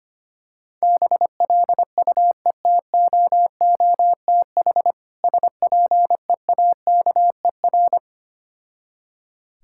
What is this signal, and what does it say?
Morse code, heard as a keyed tone, about 25 wpm, 700 Hz. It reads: BLUETOOTH SPEAKER
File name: practice / 25wpm / Difficult Words with Hints - 25wpm